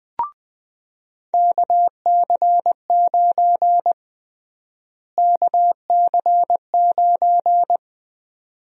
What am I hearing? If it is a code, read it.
Morse code, heard as a keyed tone, about 20 wpm, 700 Hz. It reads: KC9 KC9
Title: USA Call Sign Prefixes - Repeated 2x - 20wpm